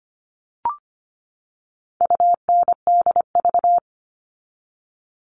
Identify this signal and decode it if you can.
Morse code, heard as a keyed tone, about 25 wpm, 700 Hz. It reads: UNDV
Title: All Cut Numbers - 4-Digit 25wpm